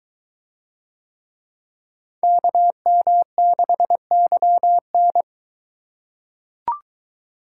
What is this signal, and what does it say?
Morse code, heard as a keyed tone, about 23 wpm, 700 Hz. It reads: KM6YN